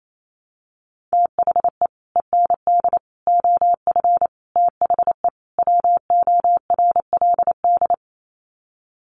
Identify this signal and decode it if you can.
Morse code, heard as a keyed tone, about 28 wpm, 700 Hz. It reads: THE END OF THE WORLD
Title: Sets of 5 Words 28wpm